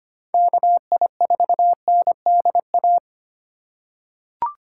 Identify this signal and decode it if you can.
Morse code, heard as a keyed tone, about 25 wpm, 700 Hz. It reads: KI4NDA
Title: USA Call Signs - 3x Character Spacing - 25wpm